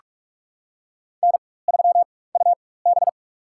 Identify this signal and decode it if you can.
Morse code, heard as a keyed tone, about 46 wpm, spaced as for 21 wpm, 700 Hz. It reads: N3UB
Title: USA Call Signs - 4 Characters - 4x Character Spacing - 45wpm